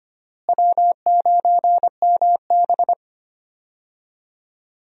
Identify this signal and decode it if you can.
Morse code, heard as a keyed tone, about 25 wpm, 700 Hz. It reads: W9MB
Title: USA Call Signs - 4 Characters - Repeated 3x - 25wpm